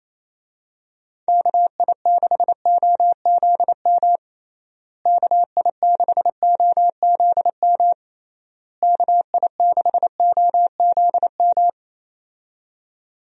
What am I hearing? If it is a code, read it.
Morse code, heard as a keyed tone, about 28 wpm, 700 Hz. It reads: KI6OZM KI6OZM KI6OZM
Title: USA Call Signs - 6 Characters - Repeated 3x - 28wpm